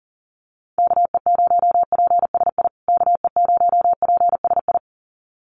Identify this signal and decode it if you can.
Morse code, heard as a keyed tone, about 40 wpm, 700 Hz. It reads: KE0PSI KE0PSI
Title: USA Call Signs Repeated 2x 40wpm